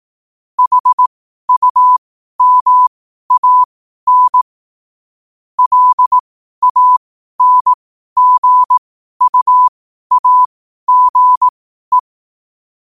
Morse code, heard as a keyed tone, about 18 wpm, spaced as for 12 wpm, 1000 Hz. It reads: HUMAN LANGUAGE
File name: qotdfiles / 12wpm